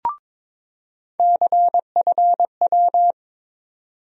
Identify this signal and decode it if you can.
Morse code, heard as a keyed tone, about 22 wpm, 700 Hz. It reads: CFW
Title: USA Call Sign Postfixes - 3 Letters 22wpm